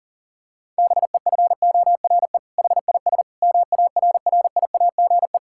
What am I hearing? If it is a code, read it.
Morse code, heard as a keyed tone, about 40 wpm, 700 Hz. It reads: BEFORE HIS MARRIAGE
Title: Sentences from Top 1000 Words - Review Part 5 - 40wpm